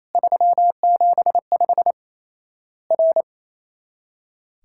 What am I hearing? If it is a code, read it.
Morse code, heard as a keyed tone, about 28 wpm, 700 Hz. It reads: 375 R
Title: Worked All Europe DX Contest - QTC Exchange - 28wpm